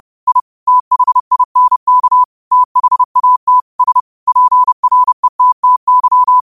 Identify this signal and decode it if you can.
Morse code, heard as a keyed tone, about 30 wpm, 1000 Hz. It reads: I THINK THAT S PRETTY
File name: qotdfiles / 30wpm